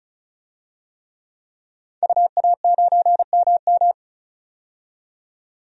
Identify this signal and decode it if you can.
Morse code, heard as a keyed tone, about 34 wpm, 700 Hz. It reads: UA9MM
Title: DXCC-100 Call Signs - 5 Characters - 35wpm